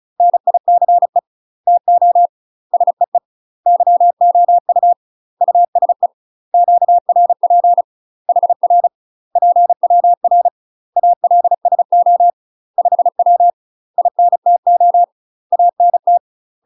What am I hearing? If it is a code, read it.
Morse code, heard as a keyed tone, about 36 wpm, 700 Hz. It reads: NICE TO SEE YOU USE QRP HR PWR ALSO 5W INTO ANT
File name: UK MASTER QSOs TRAINING / 35WPM / UK MASTER QSOs 0023